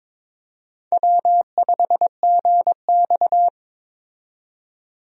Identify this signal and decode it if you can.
Morse code, heard as a keyed tone, about 22 wpm, 700 Hz. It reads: W5GX
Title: USA Call Signs - 4 Characters - 22wpm